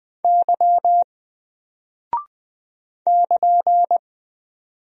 Morse code, heard as a keyed tone, about 20 wpm, 700 Hz. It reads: Y <KN>
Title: Single Character (Letter:Number:Punctuation:Prosign) 20wpm